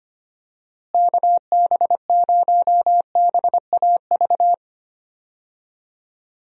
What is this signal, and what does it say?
Morse code, heard as a keyed tone, about 25 wpm, 700 Hz. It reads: KB0BAV
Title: USA Call Signs - 6 Characters - Repeated 2x - 25wpm